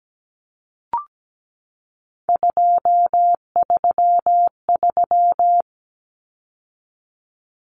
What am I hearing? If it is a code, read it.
Morse code, heard as a keyed tone, about 17 wpm, 700 Hz. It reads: 233
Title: Common Cut Numbers - 3-Digit 17wpm